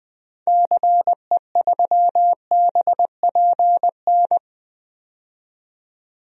Morse code, heard as a keyed tone, about 20 wpm, 700 Hz. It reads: CE3BPN
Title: DXCC-100 Call Signs - 6 Characters - 20wpm